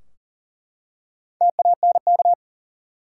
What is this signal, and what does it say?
Morse code, heard as a keyed tone, about 40 wpm, 700 Hz. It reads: TANK
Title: Four Letter Words - 40wpm